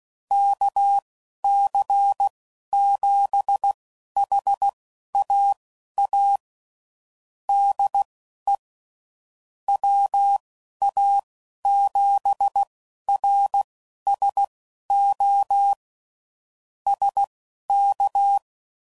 Morse code, heard as a keyed tone, about 16 wpm, spaced as for 11 wpm, 800 Hz. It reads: KC7HAA DE WA7RSO SK
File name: CW Code Groups @10WPM / QSOs / QSO kc7haa